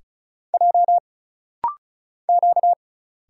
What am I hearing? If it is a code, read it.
Morse code, heard as a keyed tone, about 34 wpm, 700 Hz. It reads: J Q
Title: Single Letters - Rapid-Fire 35wpm